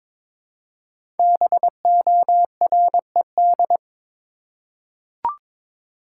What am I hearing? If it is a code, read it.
Morse code, heard as a keyed tone, about 22 wpm, 700 Hz. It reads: BORED
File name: 5 Letter Words - First 3 Letters Given - 22wpm